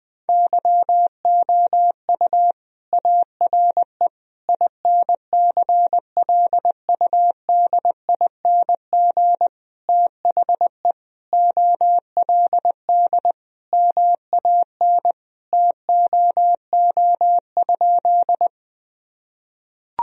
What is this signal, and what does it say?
Morse code, heard as a keyed tone, about 21 wpm, 700 Hz. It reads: YOU ARE INCLUDING THE OLD MAN TOO?
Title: Sentences from Top 1000 Words - Review Part 4 - 20wpm